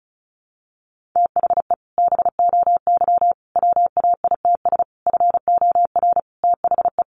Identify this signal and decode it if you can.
Morse code, heard as a keyed tone, about 34 wpm, 700 Hz. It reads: THE BOY WAITS FOR THE
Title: Easy Phrases 35wpm